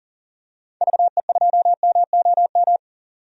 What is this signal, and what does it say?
Morse code, heard as a keyed tone, about 40 wpm, 700 Hz. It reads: VE2MOM
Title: DXCC-100 Call Signs - 6 Characters - 40wpm